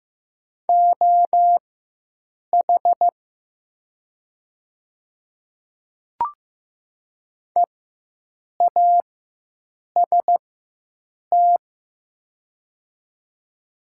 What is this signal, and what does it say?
Morse code, heard as a keyed tone, about 15 wpm, spaced as for 7 wpm, 700 Hz. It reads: OH EAST
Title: 4x Character Spacing - Top 500 Words - 15wpm